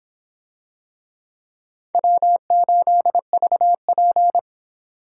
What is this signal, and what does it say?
Morse code, heard as a keyed tone, about 25 wpm, 700 Hz. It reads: W8VP